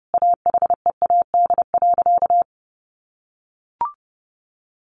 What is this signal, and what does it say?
Morse code, heard as a keyed tone, about 30 wpm, 700 Hz. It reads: AHEAD.
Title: Sentences from Top 1100 Words 30wpm